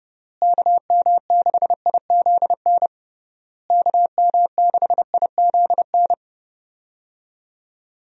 Morse code, heard as a keyed tone, about 30 wpm, 700 Hz. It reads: KM6IZN KM6IZN
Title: USA Call Signs - 6 Characters - Repeated 3x - 30wpm